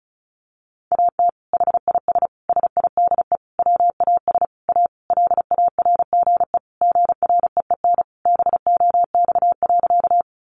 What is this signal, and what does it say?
Morse code, heard as a keyed tone, about 34 wpm, 700 Hz. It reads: AT HIS SIDE WAS A LARGE GREEN BOX.
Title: Sentences from Top 500 Words 35wpm